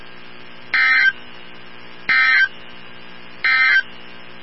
Emergency Alert System (SAME) tones, heard as harsh buzzing bursts. It is heard from 0.6 to 3.9 s.
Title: NOAA Weather Radio 162.550